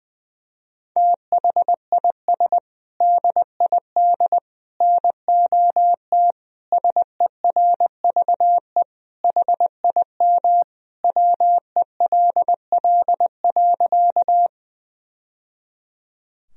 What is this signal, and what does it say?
Morse code, heard as a keyed tone, about 19 wpm, 700 Hz. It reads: THIS DID NOT SERVE HIM WELL.
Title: Sentences from Top 1000 Words - Review Part 2 - 20wpm